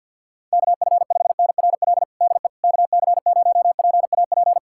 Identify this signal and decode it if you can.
Morse code, heard as a keyed tone, about 49 wpm, 700 Hz. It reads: KF5NRL DE KC0PAP
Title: USA Callsign DE Callsign 50wpm